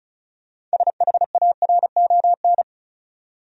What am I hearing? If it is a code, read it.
Morse code, heard as a keyed tone, about 34 wpm, 700 Hz. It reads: SHARON
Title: USA Common Names - 35wpm